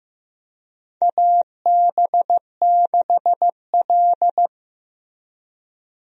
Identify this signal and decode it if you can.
Morse code, heard as a keyed tone, about 15 wpm, 700 Hz. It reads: AB6L
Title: USA Call Signs - 4 Characters - Repeated 3x - 15wpm